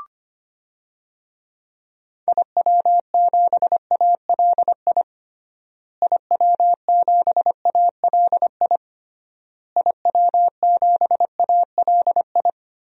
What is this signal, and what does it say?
Morse code, heard as a keyed tone, about 25 wpm, 700 Hz. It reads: IW7ALI IW7ALI IW7ALI
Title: DXCC-100 Call Signs - 6 Characters - Repeated 3x - 25wpm